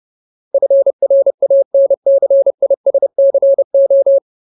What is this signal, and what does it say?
Morse code, heard as a keyed tone, about 30 wpm, 550 Hz. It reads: FRANCISCO
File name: Ultimate Word List / 30WPM Ultimate